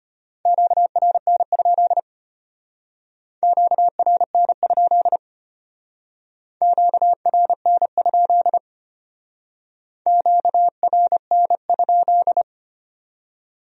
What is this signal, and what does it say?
Morse code, heard as a keyed tone, about 32 wpm, 700 Hz. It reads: QRN? QRN? QRN? QRN?